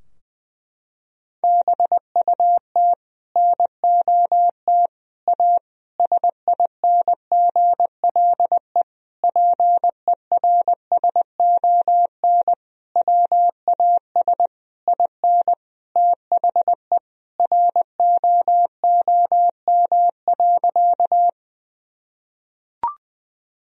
Morse code, heard as a keyed tone, about 20 wpm, 700 Hz. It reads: BUT NOT A SINGLE PERSON WAS IN THE ROOM.
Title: Sentences from Top 1000 Words - Review Part 4 - 20wpm